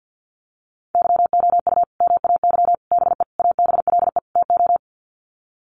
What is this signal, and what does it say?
Morse code, heard as a keyed tone, about 50 wpm, 700 Hz. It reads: YOU MAY BE ABLE TO
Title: Sets of 5 Words Encore 50wpm